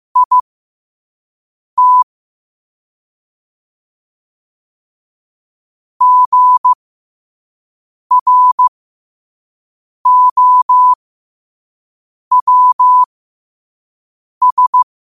Morse code, heard as a keyed tone, about 15 wpm, spaced as for 5 wpm, 1000 Hz. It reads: IT GROWS
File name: qotdfiles / 5wpm